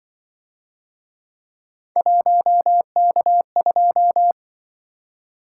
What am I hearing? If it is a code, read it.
Morse code, heard as a keyed tone, about 25 wpm, 700 Hz. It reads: 1K2
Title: Speed-Racing - Three Letter-Number Combinations - 20wpm